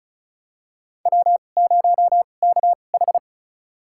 Morse code, heard as a keyed tone, about 36 wpm, spaced as for 25 wpm, 700 Hz. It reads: W0KH
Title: USA Call Signs - 4 Characters - 2x Character Spacing - 35wpm